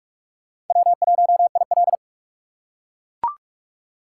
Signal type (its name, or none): Morse code